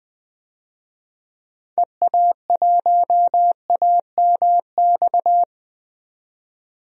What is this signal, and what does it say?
Morse code, heard as a keyed tone, about 20 wpm, 700 Hz. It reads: EA1AMX
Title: DXCC-100 Call Signs - 6 Characters - 20wpm